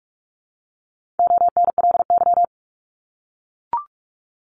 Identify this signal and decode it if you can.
Morse code, heard as a keyed tone, about 44 wpm, 700 Hz. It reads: ONLY
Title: Top 200 Words 45wpm